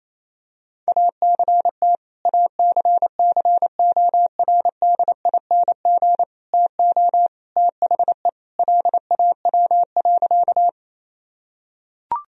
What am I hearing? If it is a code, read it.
Morse code, heard as a keyed tone, about 28 wpm, 700 Hz. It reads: ACT ACCORDING TO THE LAW.
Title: Sentences from Top 1000 Words - Review Part 4 - 28wpm